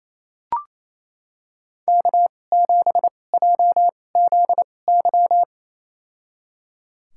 Morse code, heard as a keyed tone, about 28 wpm, spaced as for 20 wpm, 700 Hz. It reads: K7JZY
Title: USA Call Signs - 2x Character Spacing - 28wpm